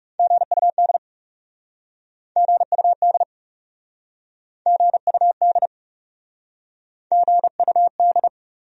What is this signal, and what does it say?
Morse code, heard as a keyed tone, about 38 wpm, 700 Hz. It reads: GUD GUD GUD GUD